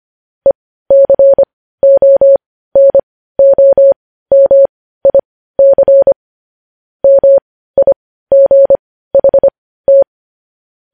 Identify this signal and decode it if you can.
Morse code, heard as a keyed tone, about 25 wpm, spaced as for 15 wpm, 550 Hz. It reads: ECONOMIC MIGHT
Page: news-headlines-in-morse-code-at-15-wpm